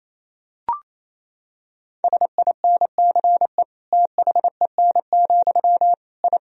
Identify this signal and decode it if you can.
Morse code, heard as a keyed tone, about 28 wpm, 700 Hz. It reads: SINCE THEN, I
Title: Sentences from Top 1000 Words - Review Part 1 - 28wpm